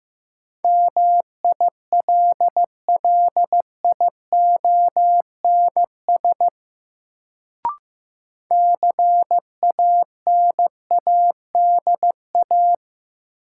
Morse code, heard as a keyed tone, about 15 wpm, 700 Hz. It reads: MILLIONS CANADA
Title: Sets of 5 Words - New Word Drill - 15wpm